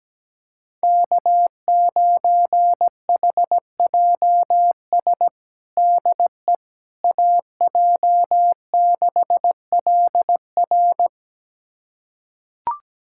Morse code, heard as a keyed tone, about 17 wpm, 700 Hz. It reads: K9HJS DE AJ6LR
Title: USA Callsign DE Callsign - 5 Characters - 17wpm